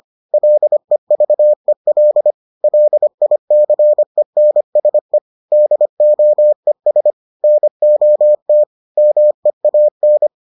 Morse code, heard as a keyed tone, about 25 wpm, 600 Hz. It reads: LEVEL LICENSE DOES NOT MEAN